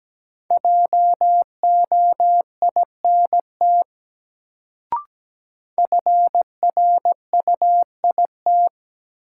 Morse code, heard as a keyed tone, about 17 wpm, 700 Hz. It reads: JOINT FRUIT
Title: Sentences from Top 1300 Words - New Word Drill - 17wpm